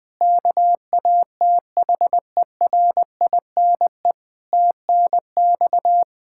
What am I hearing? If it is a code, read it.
Morse code, heard as a keyed tone, about 21 wpm, 700 Hz. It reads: KATHERINE TNX
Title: Practice QSO - Exchange 3 - More Info - 20wpm